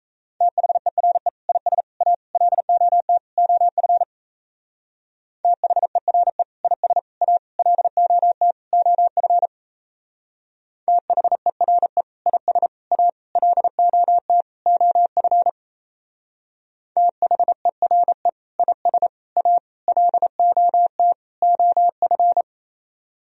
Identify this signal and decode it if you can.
Morse code, heard as a keyed tone, about 34 wpm, 700 Hz. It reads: THERE IS A LOT OF THERE IS A LOT OF THERE IS A LOT OF THERE IS A LOT OF
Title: Speed-Racing - Sets of 5 Words 28wpm